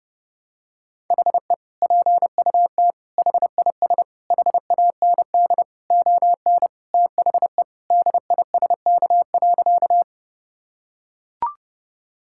Morse code, heard as a keyed tone, about 30 wpm, 700 Hz. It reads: HE PUT HIS HAND ON THE DISK.
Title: Sentences from Top 1000 Words - Review Part 6 - 30wpm